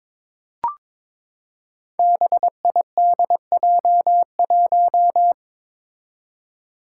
Morse code, heard as a keyed tone, about 22 wpm, 700 Hz. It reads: BIDJ1
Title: Five Challenging Letter-Number Combinations 22wpm